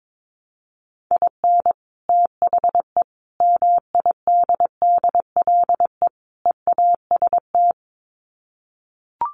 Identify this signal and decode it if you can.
Morse code, heard as a keyed tone, about 22 wpm, 700 Hz. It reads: IN THE MIDDLE EAST
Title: Sets of 4 Words 22wpm